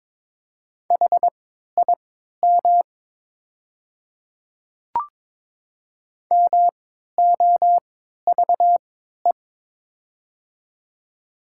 Morse code, heard as a keyed tone, about 22 wpm, spaced as for 13 wpm, 700 Hz. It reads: HIM MOVE
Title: 3x Character Spacing - Top 200 Words - 22wpm